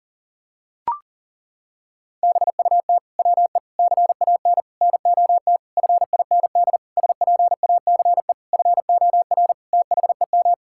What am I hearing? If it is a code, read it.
Morse code, heard as a keyed tone, about 40 wpm, 700 Hz. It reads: BUT WE CAN NOT FIND SPACE FOR THEM
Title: Sentences from Top 1000 Words - Review Part 5 - 40wpm